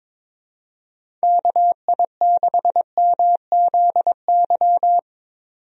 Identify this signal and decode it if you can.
Morse code, heard as a keyed tone, about 22 wpm, 700 Hz. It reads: KI6MZY
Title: USA Call Signs - 6 Characters - 22wpm